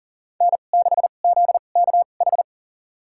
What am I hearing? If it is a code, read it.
Morse code, heard as a keyed tone, about 40 wpm, spaced as for 29 wpm, 700 Hz. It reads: N6ZKH
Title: USA Call Signs - 5 Characters - 2x Character Spacing - 40wpm